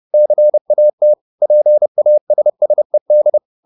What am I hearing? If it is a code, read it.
Morse code, heard as a keyed tone, about 30 wpm, 600 Hz. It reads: CAT PASSED